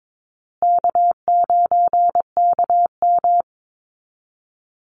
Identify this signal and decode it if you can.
Morse code, heard as a keyed tone, about 22 wpm, 700 Hz. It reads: K9KM
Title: Easy Call Signs Repeated 3x - 22wpm